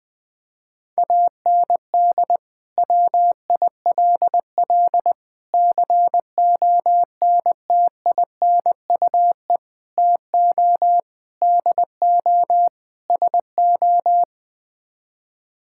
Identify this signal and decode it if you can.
Morse code, heard as a keyed tone, about 20 wpm, 700 Hz. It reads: AND WILL CONTINUE TO DO SO
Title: Sets of 6 Words - 20wpm